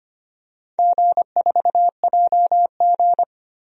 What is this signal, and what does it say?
Morse code, heard as a keyed tone, about 25 wpm, 700 Hz. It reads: G4JG
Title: DXCC-100 Call Signs - 4 Characters - 25wpm